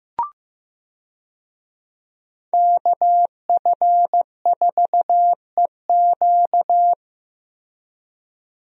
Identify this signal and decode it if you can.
Morse code, heard as a keyed tone, about 15 wpm, 700 Hz. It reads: KF4EQ
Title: USA Call Signs - 5 Characters - Repeated 2x - 15wpm